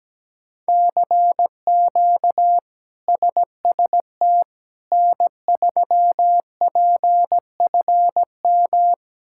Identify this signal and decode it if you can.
Morse code, heard as a keyed tone, about 17 wpm, 700 Hz. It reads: CQ SST N3PFM
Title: K1USN SST QSO Practice - 17wpm